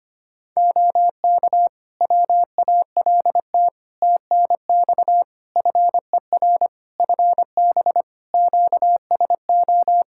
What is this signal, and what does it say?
Morse code, heard as a keyed tone, about 25 wpm, 700 Hz. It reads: OK WALT TNX FER FB QSO
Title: Practice QSO - Exchange 4 - Ending - 25wpm